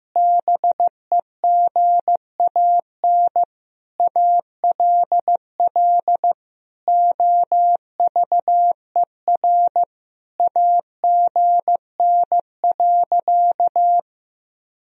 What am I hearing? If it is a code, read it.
Morse code, heard as a keyed tone, about 15 wpm, 700 Hz. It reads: BEGAN ALL OVER AGN.
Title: Sentences from Top 500 Words and Abbreviations 15wpm